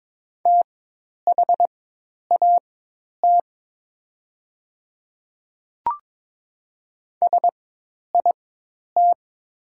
Morse code, heard as a keyed tone, about 22 wpm, spaced as for 10 wpm, 690 Hz. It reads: THAT SIT
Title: 4x Character Spacing - Top 500 Words - 22wpm